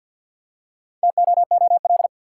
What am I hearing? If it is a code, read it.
Morse code, heard as a keyed tone, about 53 wpm, 700 Hz. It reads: TOOL
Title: Four Letter Words - 50wpm